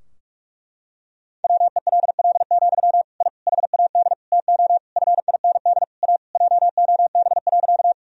Morse code, heard as a keyed tone, about 46 wpm, 700 Hz. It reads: WELL, I HAD TO FIND A JOB.
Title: Sentences from Top 1000 Words - Review Part 3 - 45wpm